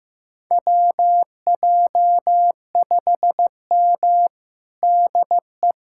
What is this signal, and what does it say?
Morse code, heard as a keyed tone, about 15 wpm, 700 Hz. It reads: WJ5M DE
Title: USA Callsign DE Callsign - 4 Characters - 15wpm